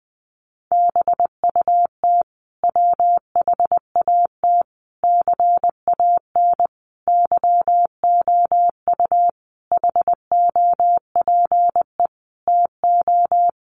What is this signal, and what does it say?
Morse code, heard as a keyed tone, about 20 wpm, 700 Hz. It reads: BUT WHAT CAN YOU HOPE TO
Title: Sentences from Top 1000 Words 20wpm